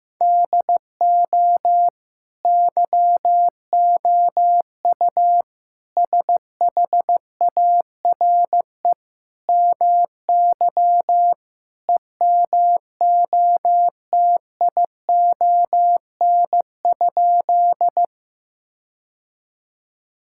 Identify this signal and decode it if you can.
Morse code, heard as a keyed tone, about 15 wpm, 700 Hz. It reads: DO YOU SHARE MY EMOTION?
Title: Sentences from Top 1600 Words 15wpm